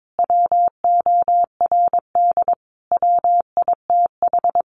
Morse code, heard as a keyed tone, about 22 wpm, 700 Hz. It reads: WORD WITH
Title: Sentences from Top 900 Words 22wpm